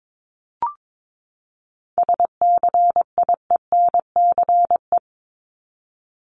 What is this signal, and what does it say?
Morse code, heard as a keyed tone, about 22 wpm, 700 Hz. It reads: SCIENCE
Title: Top 500 Words 22wpm